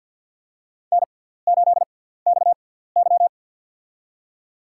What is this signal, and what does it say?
Morse code, heard as a keyed tone, about 53 wpm, spaced as for 17 wpm, 700 Hz. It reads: N8XY
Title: USA Call Signs - 4 Characters - 6x Character Spacing - 50wpm